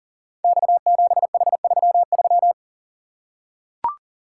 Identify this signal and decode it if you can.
Morse code, heard as a keyed tone, about 40 wpm, 700 Hz. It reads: X7H33
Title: Code Groups - 5 Letter:Numbers - 40wpm